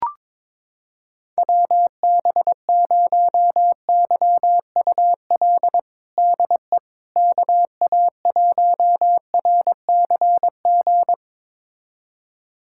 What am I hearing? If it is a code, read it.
Morse code, heard as a keyed tone, about 22 wpm, 700 Hz. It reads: WB0YUL DE KA1RCG
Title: USA Callsign DE Callsign 22wpm